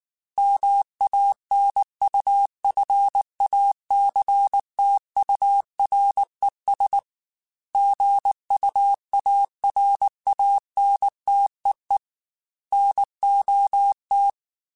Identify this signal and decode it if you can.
Morse code, heard as a keyed tone, about 19 wpm, 800 Hz. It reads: MANUFACTURES GUARANTEE NOT
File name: Chuckles / Chuckles-mp3s / Guaranteed @18